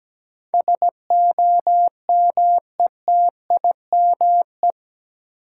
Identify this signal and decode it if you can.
Morse code, heard as a keyed tone, about 17 wpm, 700 Hz. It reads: SOMETIME
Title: Sentences from Top 1500 Words - New Word Drill Alt - 17wpm